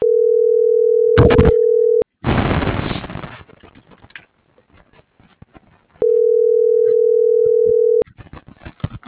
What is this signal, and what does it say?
Ringback tone (North America), 440 + 480 Hz.